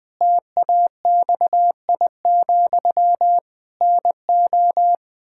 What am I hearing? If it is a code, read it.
Morse code, heard as a keyed tone, about 20 wpm, 700 Hz. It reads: TAXI, NO
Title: Pangram Challenge v2 - 20wpm